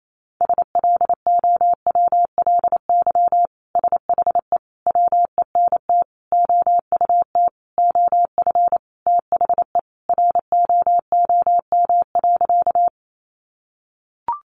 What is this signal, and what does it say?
Morse code, heard as a keyed tone, about 28 wpm, 700 Hz. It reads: SLOWLY SHE WENT OUT OF THE ROOM.
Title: Sentences from Top 1100 Words 28wpm